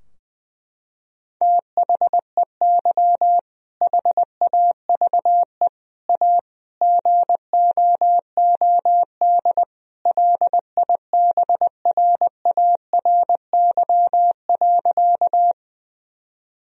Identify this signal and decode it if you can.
Morse code, heard as a keyed tone, about 20 wpm, 700 Hz. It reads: THEY HAVE A GOOD LIBRARY.
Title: Sentences from Top 1000 Words - Review Part 4 - 20wpm